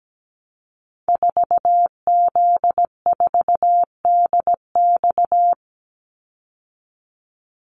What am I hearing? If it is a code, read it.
Morse code, heard as a keyed tone, about 17 wpm, 700 Hz. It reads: 4Z4DX
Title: Call Signs Repeated 2x - 17wpm